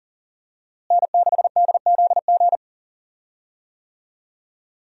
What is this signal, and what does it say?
Morse code, heard as a keyed tone, about 40 wpm, 700 Hz. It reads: N6DZG
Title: USA Call Signs - 5 Characters - Repeated 3x - 40wpm